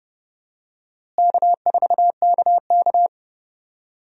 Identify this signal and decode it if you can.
Morse code, heard as a keyed tone, about 30 wpm, 700 Hz. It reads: K4KK
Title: USA Call Signs - 4 Characters - Repeated 3x - 30wpm